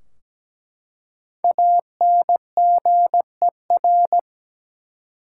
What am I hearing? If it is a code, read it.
Morse code, heard as a keyed tone, about 17 wpm, 700 Hz. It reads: ANGER
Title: Five Letter Words - 17wpm